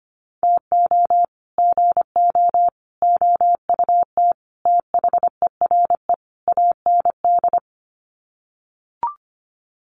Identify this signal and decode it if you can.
Morse code, heard as a keyed tone, about 25 wpm, 700 Hz. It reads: TO GO OUT THERE AND
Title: Sets of 5 Words Encore2 25wpm